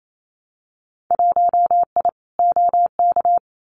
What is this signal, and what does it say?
Morse code, heard as a keyed tone, about 28 wpm, 700 Hz. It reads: 1I OK
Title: Winter Field Day Exchanges Encore 28wpm